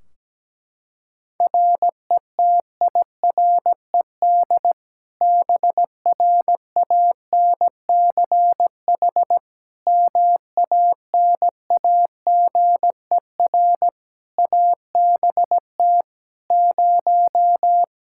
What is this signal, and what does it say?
Morse code, heard as a keyed tone, about 17 wpm, 700 Hz. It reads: RETIRED BRANCH MANAGER ABT 0
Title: Practice QSO - Exchange 3 - More Info - 17wpm